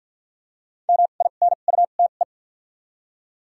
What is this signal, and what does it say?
Morse code, heard as a keyed tone, about 52 wpm, spaced as for 36 wpm, 700 Hz. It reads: MINUTE
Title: 2x Character Spacing - Top 500 Words - 50wpm